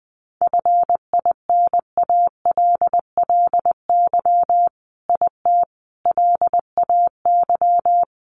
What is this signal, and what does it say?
Morse code, heard as a keyed tone, about 19 wpm, 700 Hz. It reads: FINALLY IT LAY
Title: Sentences from Top 1000 Words 20wpm